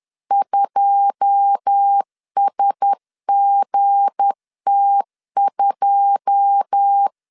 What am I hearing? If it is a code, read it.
Morse code, heard as a keyed tone, about 10 wpm, 800 Hz. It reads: 2SGT2
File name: LEZIONI / 9 2 10wpm 2736